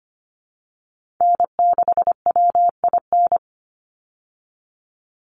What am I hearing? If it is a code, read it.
Morse code, heard as a keyed tone, about 25 wpm, 700 Hz. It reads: N6WIN
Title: Call Signs Repeated 2x - 25wpm